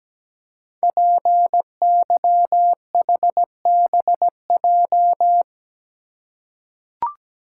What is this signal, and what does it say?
Morse code, heard as a keyed tone, about 17 wpm, 700 Hz. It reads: PYHBJ
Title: Code Groups - 5 Letters - 17wpm